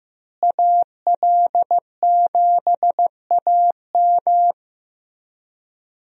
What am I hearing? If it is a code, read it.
Morse code, heard as a keyed tone, about 15 wpm, 700 Hz. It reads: AL7AM
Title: USA Call Signs - 5 Characters - Repeated 2x - 15wpm